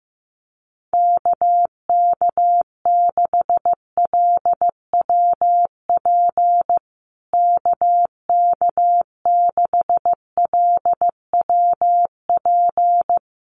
Morse code, heard as a keyed tone, about 15 wpm, 700 Hz. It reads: KK6LWP KK6LWP
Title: USA Call Signs Repeated 2x 15wpm